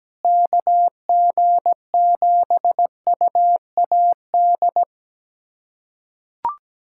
Morse code, heard as a keyed tone, about 17 wpm, 700 Hz. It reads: KG7UAD
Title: USA Callsign DE Callsign 17wpm